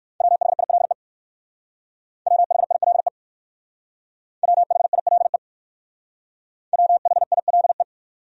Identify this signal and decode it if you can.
Morse code, heard as a keyed tone, about 56 wpm, 700 Hz. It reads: WHILE WHILE WHILE WHILE
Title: Speed-Racing Top 500 Words 45wpm